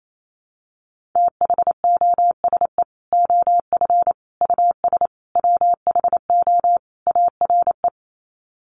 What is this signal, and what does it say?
Morse code, heard as a keyed tone, about 28 wpm, 700 Hz. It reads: THOSE OF US WHO ARE
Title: Sets of 5 Words Encore2 28wpm